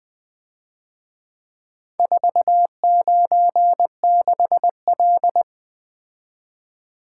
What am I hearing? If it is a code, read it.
Morse code, heard as a keyed tone, about 24 wpm, 700 Hz. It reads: 496L
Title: Speed-Racing - Four Letter-Number Combinations - 20wpm